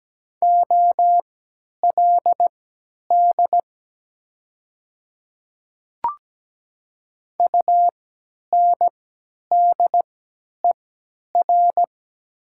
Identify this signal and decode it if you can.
Morse code, heard as a keyed tone, about 17 wpm, spaced as for 10 wpm, 700 Hz. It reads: OLD UNDER
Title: 3x Character Spacing - Top 300 Words - 17wpm